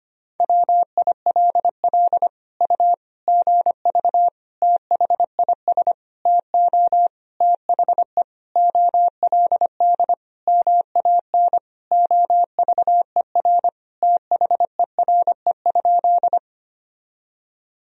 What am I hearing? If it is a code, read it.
Morse code, heard as a keyed tone, about 25 wpm, 700 Hz. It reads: WILL U GV THIS TO THE OLD MAN OVER THERE?
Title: Sentences from Top 200 Words and Abbreviations 25wpm